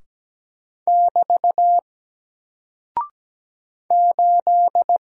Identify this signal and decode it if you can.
Morse code, heard as a keyed tone, about 17 wpm, 700 Hz. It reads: <BT> 8
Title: Single Character (Letter:Number:Punctuation:Prosign) 17wpm